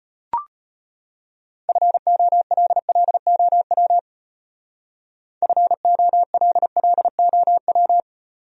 Morse code, heard as a keyed tone, about 37 wpm, 700 Hz. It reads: FOLLOW FOLLOW